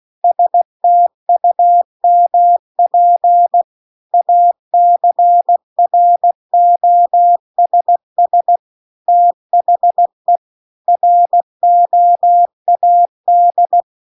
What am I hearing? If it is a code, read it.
Morse code, heard as a keyed tone, about 16 wpm, 700 Hz. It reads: STUMP ACROSS THE ROAD